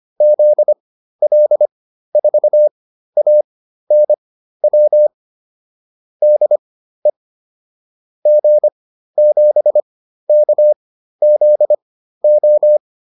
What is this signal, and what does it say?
Morse code, heard as a keyed tone, about 25 wpm, spaced as for 13 wpm, 600 Hz. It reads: ZL4ANW DE G7KZO